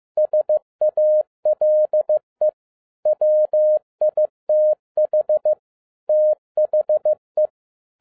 Morse code, heard as a keyed tone, about 15 wpm, 610 Hz. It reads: SALE WITH THE
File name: 2012-01-mp3 / n13-A NEWLY DESIGNED QSL CARD-15wpm